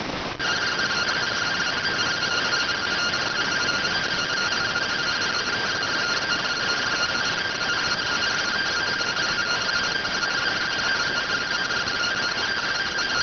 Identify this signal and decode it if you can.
Two-tone teleprinter signal, about 170 Hz shift.
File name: findings / SAGITAL fr15 nb203 3mm.gen-04 rx